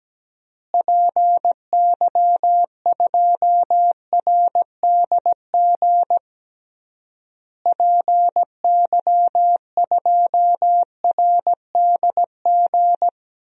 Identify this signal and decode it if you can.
Morse code, heard as a keyed tone, about 17 wpm, 700 Hz. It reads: PY2RDG PY2RDG
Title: DXCC-100 Call Signs - 6 Characters - Repeated 3x - 17wpm